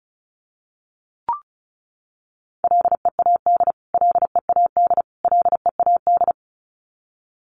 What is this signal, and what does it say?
Morse code, heard as a keyed tone, about 34 wpm, 700 Hz. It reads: LEAD LEAD LEAD
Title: Top 500 Words Repeated 35wpm